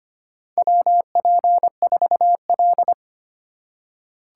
Morse code, heard as a keyed tone, about 25 wpm, 700 Hz. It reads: WP4L
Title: USA Call Signs - 4x Character Spacing - 25wpm